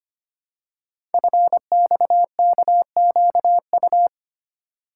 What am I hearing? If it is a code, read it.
Morse code, heard as a keyed tone, about 25 wpm, 700 Hz. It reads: FXKQU
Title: Code Groups - 5 Letters - 25wpm